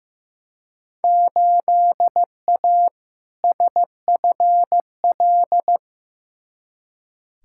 Morse code, heard as a keyed tone, about 15 wpm, 700 Hz. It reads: 8A SFL
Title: Field Day Exchanges Encore - 15wpm